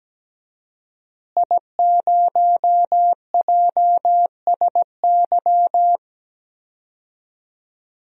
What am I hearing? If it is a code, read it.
Morse code, heard as a keyed tone, about 17 wpm, 700 Hz. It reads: I0JSY
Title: DXCC-100 Call Signs - 5 Characters - Repeated 3x - 17wpm